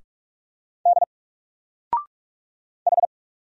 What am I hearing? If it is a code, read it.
Morse code, heard as a keyed tone, about 46 wpm, 700 Hz. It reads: D H
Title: Single Letter-Number 45wpm